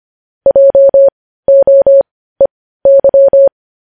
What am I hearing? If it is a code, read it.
Morse code, heard as a keyed tone, about 25 wpm, spaced as for 15 wpm, 550 Hz. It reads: JOEY
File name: podcast / News headlines-1x-25x15